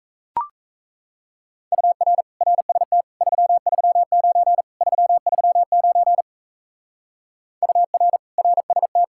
Morse code, heard as a keyed tone, about 38 wpm, 700 Hz. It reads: UR RST 339 339 UR RST